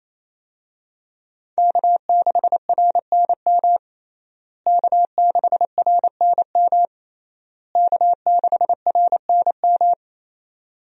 Morse code, heard as a keyed tone, about 27 wpm, 700 Hz. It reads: K6RNM K6RNM K6RNM
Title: USA Call Signs - 5 Characters - Repeated 3x - 28wpm